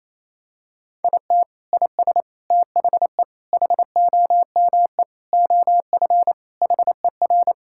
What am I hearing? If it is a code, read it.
Morse code, heard as a keyed tone, about 28 wpm, 700 Hz. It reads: IT IS THE HOME OF HER
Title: Sentences from Top 1000 Words - Review Part 5 - 28wpm